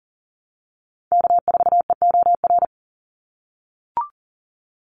Morse code, heard as a keyed tone, about 40 wpm, 700 Hz. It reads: K4EOR
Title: Call Signs - 40wpm